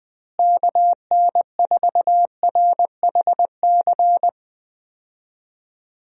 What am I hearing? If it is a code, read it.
Morse code, heard as a keyed tone, about 20 wpm, 700 Hz. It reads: KN4RHC
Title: USA Call Signs - 6 Characters - Repeated 3x - 20wpm